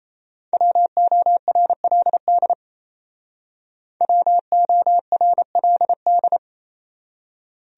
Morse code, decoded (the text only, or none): WORLD WORLD